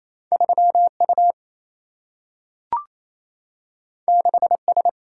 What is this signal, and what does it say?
Morse code, heard as a keyed tone, about 28 wpm, 700 Hz. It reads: 3 U 6 S
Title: Single-Letter Single-Number 28wpm